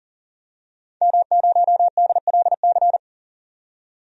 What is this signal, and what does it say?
Morse code, heard as a keyed tone, about 40 wpm, 700 Hz. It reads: M0DLC
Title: DXCC-100 Call Signs - 5 Characters - 40wpm